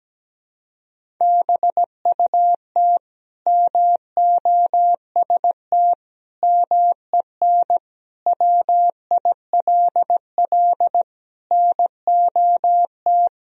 Morse code, heard as a keyed tone, about 17 wpm, 700 Hz. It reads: BUT MOST MEN WILL NOT
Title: Sentences from Top 1000 Words - Review Part 1 - 17wpm